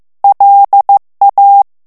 Morse code, heard as a keyed tone, about 15 wpm, 800 Hz. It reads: LA
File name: morse mp3 / FRANCAIS TEXTE CLAIR 1 VITESSE 14 MOTS MINUTES